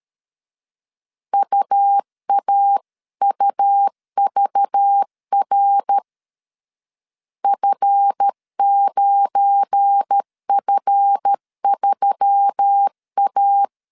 Morse code, heard as a keyed tone, about 13 wpm, 800 Hz. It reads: UAUVR F9F3A
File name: CW / DET / 1 9 1500C S A 12wpm 3800